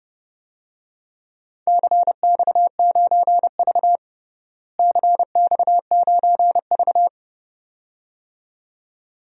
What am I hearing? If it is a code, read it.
Morse code, heard as a keyed tone, about 29 wpm, 700 Hz. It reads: CX9V CX9V
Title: DXCC-100 Call Signs - 4 Characters - Repeated 2x - 30wpm